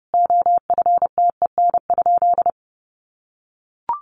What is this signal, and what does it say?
Morse code, heard as a keyed tone, about 30 wpm, 700 Hz. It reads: OFTEN?
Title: Sentences from Top 400 Words 30wpm